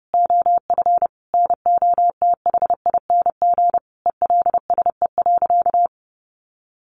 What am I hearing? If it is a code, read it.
Morse code, heard as a keyed tone, about 30 wpm, 700 Hz. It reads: OF NOTHING ELSE.
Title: Sentences from Top 800 Words 30wpm